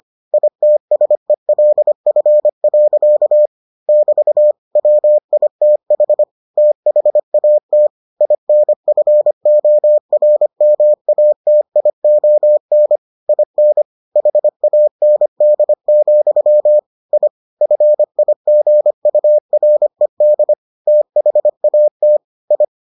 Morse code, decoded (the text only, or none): ITSELF. <BT> WITH THAT INFORMATION IN HAND, I FIGURED THAT I